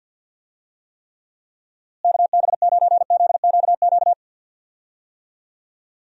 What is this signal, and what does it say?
Morse code, heard as a keyed tone, about 47 wpm, 700 Hz. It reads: KB9ZQQ
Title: USA Call Signs - 6 Characters - Repeated 2x - 50wpm